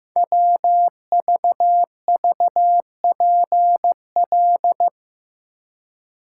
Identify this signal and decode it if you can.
Morse code, heard as a keyed tone, about 15 wpm, 700 Hz. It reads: WVVPL
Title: Code Groups - 5 Letter:Numbers - 15wpm